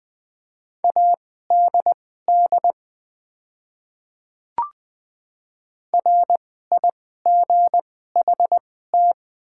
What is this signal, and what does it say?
Morse code, heard as a keyed tone, about 20 wpm, spaced as for 15 wpm, 700 Hz. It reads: ADD RIGHT
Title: 2x Character Spacing - Top 200 Words - 20wpm